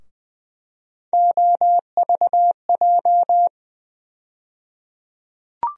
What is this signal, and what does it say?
Morse code, heard as a keyed tone, about 21 wpm, 700 Hz. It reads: OVJ
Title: USA Call Sign Postfixes - 3 Letters 20wpm